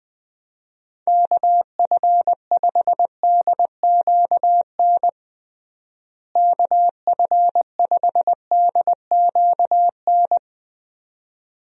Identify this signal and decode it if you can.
Morse code, heard as a keyed tone, about 20 wpm, 700 Hz. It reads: KF5DQN KF5DQN
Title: USA Call Signs - 6 Characters - Repeated 2x - 20wpm